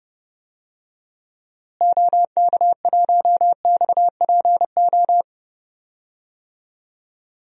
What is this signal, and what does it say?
Morse code, heard as a keyed tone, about 32 wpm, 700 Hz. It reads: OK1XPO
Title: DXCC-100 Call Signs - 6 Characters - Repeated 2x - 30wpm